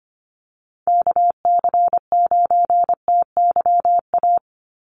Morse code, heard as a keyed tone, about 25 wpm, 700 Hz. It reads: KC9TYA
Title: NA SOTA Chaser Callsigns 25wpm